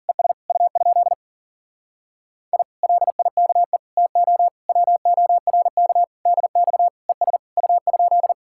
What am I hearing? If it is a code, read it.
Morse code, heard as a keyed tone, about 52 wpm, 700 Hz. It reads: ES U? I LIKE TO WORK DX ES U?